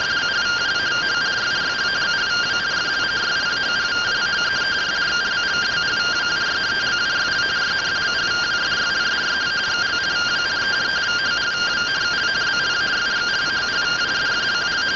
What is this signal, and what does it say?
Two-tone teleprinter signal, about 170 Hz shift.